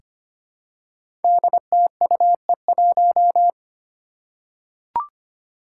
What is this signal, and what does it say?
Morse code, heard as a keyed tone, about 25 wpm, 700 Hz. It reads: DTUE1
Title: Five Challenging Letter-Number Combinations 25wpm